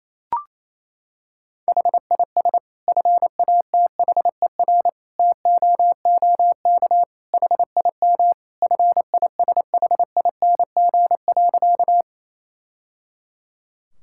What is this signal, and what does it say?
Morse code, heard as a keyed tone, about 28 wpm, 700 Hz. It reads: HIS FATHER TOOK HIM FISHING.
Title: Sentences from Top 1000 Words - Review Part 5 - 28wpm